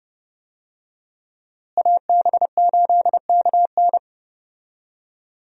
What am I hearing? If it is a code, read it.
Morse code, heard as a keyed tone, about 30 wpm, 700 Hz. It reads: AB8KN
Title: USA Call Signs - 5 Characters - Repeated 2x - 30wpm